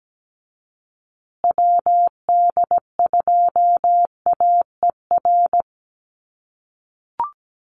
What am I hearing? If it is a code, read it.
Morse code, heard as a keyed tone, about 17 wpm, 700 Hz. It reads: WD2AER
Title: USA Call Signs Repeated 3x 17wpm